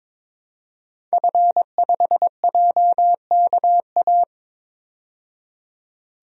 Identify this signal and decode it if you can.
Morse code, heard as a keyed tone, about 22 wpm, 700 Hz. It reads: F5JKA
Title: DXCC-100 Call Signs - 5 Characters - 22wpm